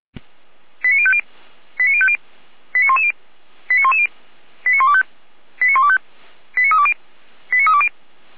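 Five-tone selective call (ZVEI), heard as quick runs of five tones, tones of about 70 ms. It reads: 89050 89050 89110 89110 89125 89125 89320 89320
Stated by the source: Five-tone ZVEI selective call 89173 received on SDR, from feuerwehr-orlamuende.org fire brigade audio archive.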